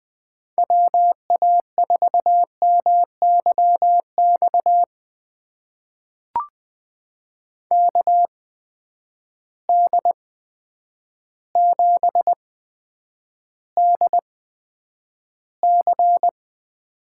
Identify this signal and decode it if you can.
Morse code, heard as a keyed tone, about 20 wpm, spaced as for 9 wpm, 700 Hz. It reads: WA4MYX KD7DC
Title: USA Call Signs - 6 Characters - 8x Character Spacing - 20wpm